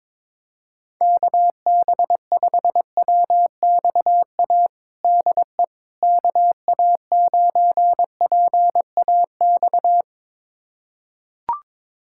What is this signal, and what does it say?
Morse code, heard as a keyed tone, about 22 wpm, 700 Hz. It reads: KB5WXA DE KA9PAX
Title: USA Callsign DE Callsign - 6 Characters - 22wpm